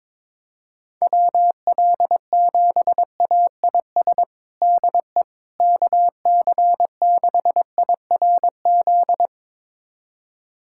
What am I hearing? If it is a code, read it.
Morse code, heard as a keyed tone, about 22 wpm, 700 Hz. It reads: WL7AIS DE KC6IRZ
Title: USA Callsign DE Callsign 22wpm